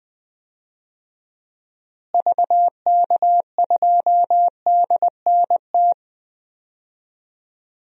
Morse code, heard as a keyed tone, about 21 wpm, 700 Hz. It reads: VK2DNT
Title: DXCC-100 Call Signs - 6 Characters - Repeated 3x - 20wpm